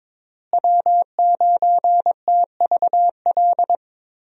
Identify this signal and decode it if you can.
Morse code, heard as a keyed tone, about 22 wpm, 700 Hz. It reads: W9TVL
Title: USA Call Signs - 5 Characters - Repeated 2x - 22wpm